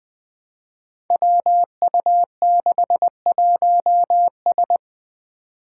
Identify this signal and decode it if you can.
Morse code, heard as a keyed tone, about 20 wpm, 700 Hz. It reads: WU61S
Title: Five Challenging Letter-Number Combinations 20wpm